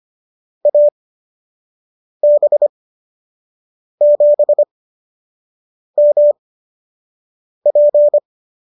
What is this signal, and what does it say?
Morse code, heard as a keyed tone, about 25 wpm, spaced as for 6 wpm, 590 Hz. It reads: AB7MP